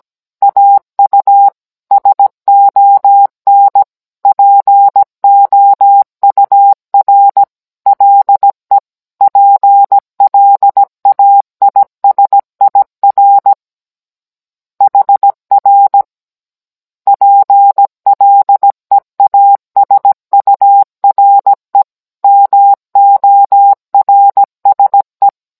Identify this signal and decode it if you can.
Morse code, heard as a keyed tone, about 17 wpm, 800 Hz. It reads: AU SON POUR LE PLAISIR HR PLEASURE MORSE